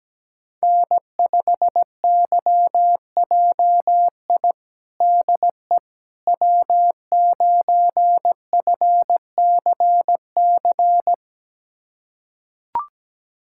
Morse code, heard as a keyed tone, about 17 wpm, 700 Hz. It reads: N5YJI DE W9FCC
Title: USA Callsign DE Callsign - 5 Characters - 17wpm